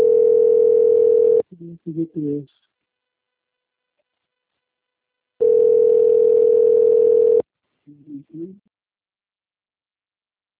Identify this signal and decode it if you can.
Ringback tone (North America), 440 + 480 Hz, with speech.
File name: out- - - - - .11911